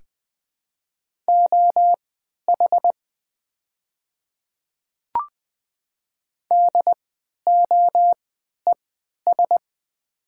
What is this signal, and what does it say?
Morse code, heard as a keyed tone, about 20 wpm, spaced as for 11 wpm, 700 Hz. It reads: OH DOES
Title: 3x Character Spacing - Top 500 Words - 20wpm